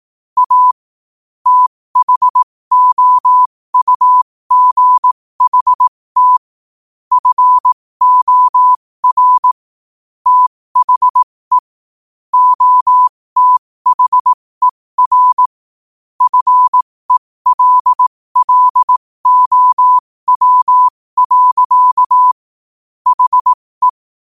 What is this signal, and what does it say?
Morse code, heard as a keyed tone, about 18 wpm, spaced as for 15 wpm, 1000 Hz. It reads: A THOUGHT FOR THE OTHER FELLOW. HE
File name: qotdfiles / 15wpm